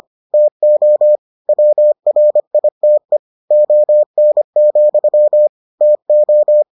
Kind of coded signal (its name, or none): Morse code